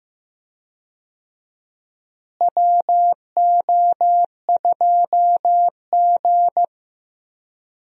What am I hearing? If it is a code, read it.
Morse code, heard as a keyed tone, about 15 wpm, 700 Hz. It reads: WO2G
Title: USA Call Signs - 4 Characters - Repeated 2x - 15wpm